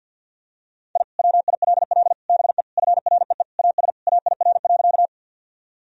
Morse code, heard as a keyed tone, about 47 wpm, 700 Hz. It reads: I WILL BE FREE AS AIR.
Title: Sentences from Top 1000 Words - Review Part 1 - 50wpm